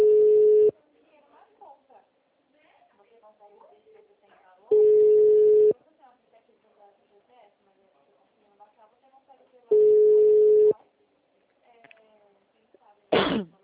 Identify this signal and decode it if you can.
Ringback tone (Europe, 425 Hz), with speech.